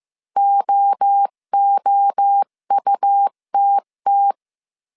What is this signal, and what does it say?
Morse code, heard as a keyed tone, about 15 wpm, 800 Hz. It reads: OOUTT
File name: CW / DET / 1 3 1500C S A 14wpm 2447